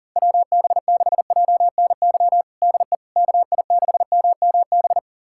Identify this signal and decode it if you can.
Morse code, heard as a keyed tone, about 40 wpm, 700 Hz. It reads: WB6JNY DE KI6MMB
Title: USA Callsign DE Callsign 40wpm